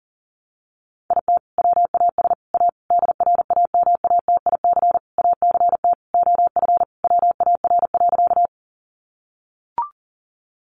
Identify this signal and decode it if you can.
Morse code, heard as a keyed tone, about 40 wpm, 700 Hz. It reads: IT WAS A DRAMATIC ACT OF WAR.
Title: Sentences from Top 1300 Words 40wpm